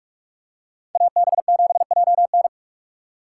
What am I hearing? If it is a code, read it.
Morse code, heard as a keyed tone, about 44 wpm, 700 Hz. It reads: AB7JN